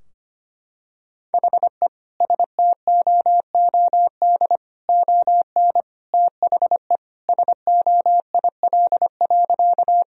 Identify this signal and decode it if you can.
Morse code, heard as a keyed tone, about 25 wpm, 700 Hz. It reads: HE STOOD ON THE SOIL.
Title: Sentences from Top 1000 Words - Review Part 4 - 25wpm